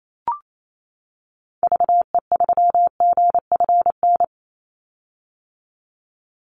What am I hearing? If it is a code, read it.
Morse code, heard as a keyed tone, about 28 wpm, 700 Hz. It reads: VE3GFN
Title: Call Signs - 28wpm